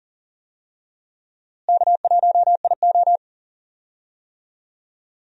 Morse code, heard as a keyed tone, about 40 wpm, 700 Hz. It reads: K1IO
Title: USA Call Signs - 4 Characters - Repeated 2x - 40wpm